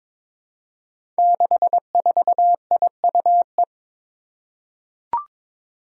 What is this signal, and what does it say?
Morse code, heard as a keyed tone, about 22 wpm, 700 Hz. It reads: 64IUE
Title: Five Challenging Letter-Number Combinations 22wpm